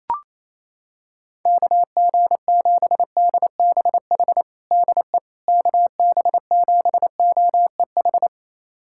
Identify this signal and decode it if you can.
Morse code, heard as a keyed tone, about 28 wpm, 700 Hz. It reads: KG7DBH DE KB7OEH
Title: USA Callsign DE Callsign - 6 Characters - 28wpm